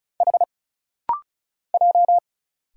Morse code, heard as a keyed tone, about 34 wpm, 700 Hz. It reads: H J
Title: Single Letter-Number - Rapid-Fire 35wpm